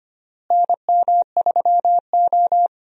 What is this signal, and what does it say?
Morse code, heard as a keyed tone, about 25 wpm, 700 Hz. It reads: NM3O
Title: USA Call Signs - 4 Characters - Repeated 3x - 25wpm